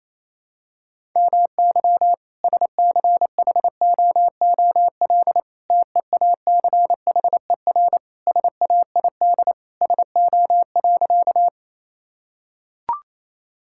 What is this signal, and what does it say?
Morse code, heard as a keyed tone, about 28 wpm, 700 Hz. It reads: MY SCHOOL TEACHER SAID SO.
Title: Sentences from Top 1000 Words - Review Part 6 - 28wpm